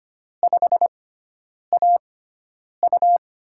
Morse code, heard as a keyed tone, about 25 wpm, spaced as for 9 wpm, 700 Hz. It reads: 5AU
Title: USA Call Signs - 5 Characters - 6x Character Spacing - 25wpm